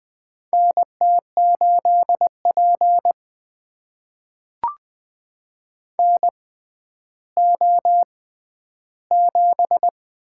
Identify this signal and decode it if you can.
Morse code, heard as a keyed tone, about 20 wpm, spaced as for 11 wpm, 700 Hz. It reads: NT8P NO7
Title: USA Call Signs - 4 Characters - 6x Character Spacing - 20wpm